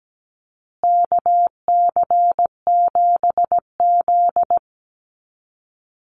Morse code, heard as a keyed tone, about 17 wpm, 700 Hz. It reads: KC7Z
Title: USA Call Signs Repeated 3x 17wpm